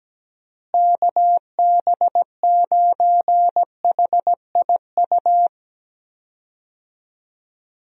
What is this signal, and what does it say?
Morse code, heard as a keyed tone, about 17 wpm, 700 Hz. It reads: KB9HIU
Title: USA Call Signs - 6 Characters - Repeated 3x - 17wpm